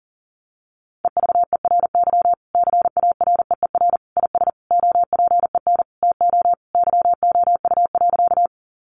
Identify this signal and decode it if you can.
Morse code, heard as a keyed tone, about 40 wpm, 700 Hz. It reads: EVERY CAREER IS OPEN TO YOU.
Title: Sentences from Top 1000 Words 40wpm